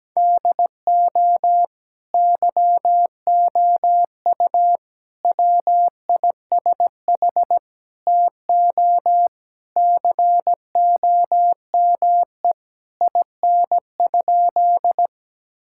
Morse code, heard as a keyed tone, about 17 wpm, 700 Hz. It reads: DO YOU WISH TO COME IN?
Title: Sentences from Top 1600 Words 17wpm